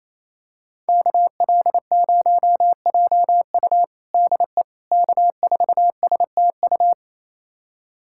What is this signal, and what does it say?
Morse code, heard as a keyed tone, about 28 wpm, 700 Hz. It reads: KL0JU DE K4STU
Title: USA Callsign DE Callsign - 5 Characters - 28wpm